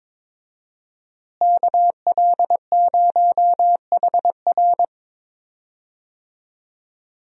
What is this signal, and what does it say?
Morse code, heard as a keyed tone, about 22 wpm, 700 Hz. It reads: KL0HR
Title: USA Call Signs - 5 Characters - Repeated 3x - 22wpm